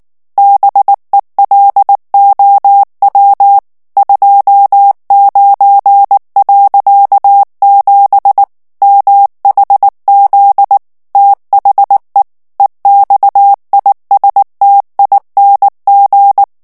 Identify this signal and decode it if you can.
Morse code, heard as a keyed tone, about 19 wpm, 800 Hz. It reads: BELOW 29.7 MHZ THE EXISTING
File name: ANGLAIS texte clair 3 vitesse 18 mots minute